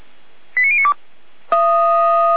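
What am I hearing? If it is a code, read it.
Five-tone selective call (ZVEI), heard as quick runs of five tones, tones of about 70 ms. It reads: 89092
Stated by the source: Received on SDR; ZVEI five-tone call.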